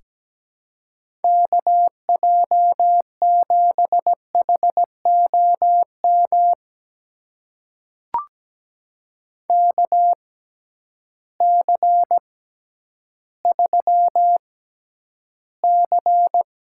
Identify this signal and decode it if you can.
Morse code, heard as a keyed tone, about 17 wpm, spaced as for 10 wpm, 700 Hz. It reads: KJ7HOM KC3C
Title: USA Call Signs - 6 Characters - 6x Character Spacing - 17wpm